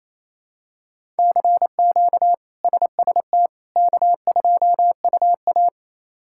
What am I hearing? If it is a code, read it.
Morse code, heard as a keyed tone, about 28 wpm, 700 Hz. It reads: CQ SST K2UA
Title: K1USN SST QSO Practice - 28wpm